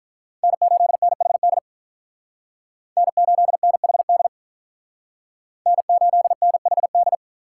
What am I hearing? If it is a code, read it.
Morse code, heard as a keyed tone, about 45 wpm, 700 Hz. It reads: N8NHD N8NHD N8NHD